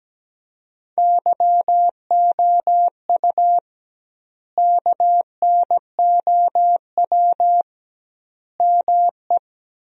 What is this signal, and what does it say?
Morse code, heard as a keyed tone, about 17 wpm, 700 Hz. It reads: YOU KNOW ME
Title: 2x Word-Spacing - Sentences from Top 400 Words - 17wpm